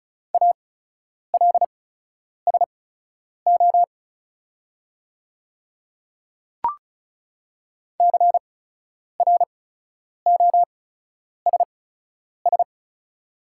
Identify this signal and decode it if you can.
Morse code, heard as a keyed tone, about 34 wpm, spaced as for 10 wpm, 700 Hz. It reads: ALSO CROSS
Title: 8x Character Spacing - Top 300 Words - 35wpm